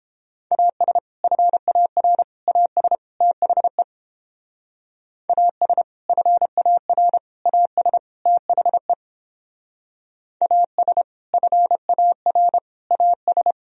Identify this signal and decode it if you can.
Morse code, heard as a keyed tone, about 29 wpm, 700 Hz. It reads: AS FAR AS THE AS FAR AS THE AS FAR AS